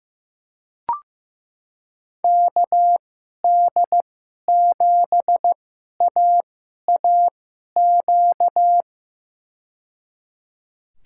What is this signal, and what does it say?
Morse code, heard as a keyed tone, about 15 wpm, spaced as for 11 wpm, 700 Hz. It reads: KD7AAQ
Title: USA Call Signs - 2x Character Spacing - 15wpm